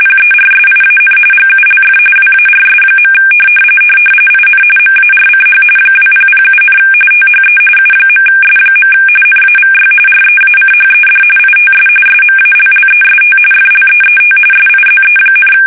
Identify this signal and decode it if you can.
RTTY, about 45.45 baud. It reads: PERSIAN GULF WARS. IN MARCH 2007, THE U.S. NAVY REACHED ITS SMALLEST FLEET SIZE,
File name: fsk850 / usnavy1